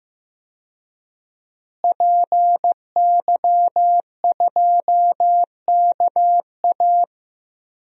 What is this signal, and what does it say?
Morse code, heard as a keyed tone, about 15 wpm, 700 Hz. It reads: PY2KA
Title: DXCC-100 Call Signs - 5 Characters - Repeated 2x - 15wpm